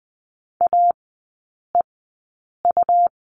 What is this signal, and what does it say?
Morse code, heard as a keyed tone, about 20 wpm, spaced as for 8 wpm, 700 Hz. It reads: AEU
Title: 107 Lesson 18e - Letter m - Three Characters 20wpm